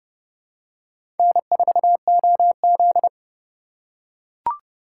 Morse code, heard as a keyed tone, about 25 wpm, 700 Hz. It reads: N4OZ
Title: Speed Racing - USA 1x2 Call Signs 20wpm